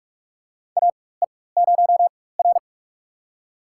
Morse code, heard as a keyed tone, about 44 wpm, spaced as for 21 wpm, 690 Hz. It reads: AE0R
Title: USA Call Signs - 4 Characters - 4x Character Spacing - 45wpm